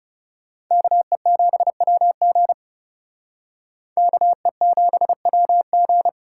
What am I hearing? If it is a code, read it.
Morse code, heard as a keyed tone, about 39 wpm, 700 Hz. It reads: KE7WG KE7WG